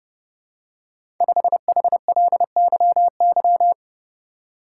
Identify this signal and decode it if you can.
Morse code, heard as a keyed tone, about 29 wpm, 700 Hz. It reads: 5HLYY
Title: Code Groups - 5 Letter:Numbers - 30wpm